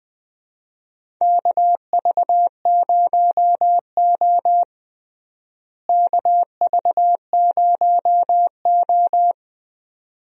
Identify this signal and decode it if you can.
Morse code, heard as a keyed tone, about 21 wpm, 700 Hz. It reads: KV0O KV0O
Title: USA Call Signs - 4 Characters - Repeated 2x - 20wpm